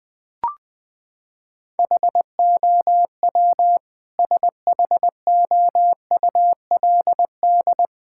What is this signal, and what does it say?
Morse code, heard as a keyed tone, about 20 wpm, 700 Hz. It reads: HOW SHOULD
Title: Sentences from Top 1000 Words - Review Part 6 - 20wpm